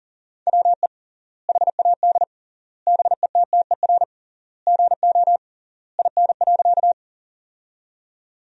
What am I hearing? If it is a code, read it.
Morse code, heard as a keyed tone, about 40 wpm, 700 Hz. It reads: WE HAD BETTER GO IN.
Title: 3x Word-Spacing - Sentences from Top 400 Words - 40wpm